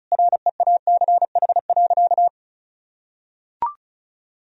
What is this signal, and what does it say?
Morse code, heard as a keyed tone, about 34 wpm, 700 Hz. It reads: REACH.
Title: Sentences from Top 1000 Words - Review Part 2 - 35wpm